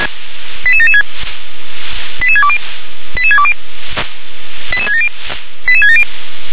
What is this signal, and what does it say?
Five-tone selective call (ZVEI), heard as quick runs of five tones, tones of about 70 ms. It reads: 88776 88620 88620 88670 88670